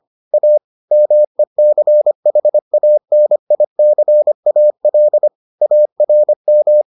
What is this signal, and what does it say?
Morse code, heard as a keyed tone, about 25 wpm, 600 Hz. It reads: A MECHANICAL ARM